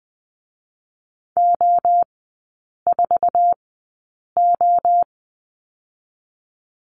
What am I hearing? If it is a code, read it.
Morse code, heard as a keyed tone, about 20 wpm, 700 Hz. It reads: O 4 O
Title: 047 Lesson 9e - Number 4 - Three Characters 20wpm